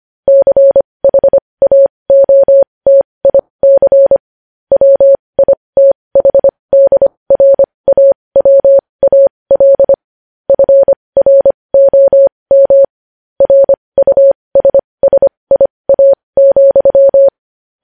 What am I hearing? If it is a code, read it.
Morse code, heard as a keyed tone, about 25 wpm, spaced as for 20 wpm, 550 Hz. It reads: CHAOTIC WITHDRAWAL FROM RUSSIA,
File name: podcast / News headlines-1x-25x20